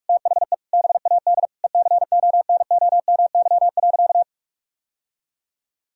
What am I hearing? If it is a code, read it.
Morse code, heard as a keyed tone, about 41 wpm, 700 Hz. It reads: THE BAD ECONOMY.
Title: Sentences from Top 1000 Words - Review Part 3 - 45wpm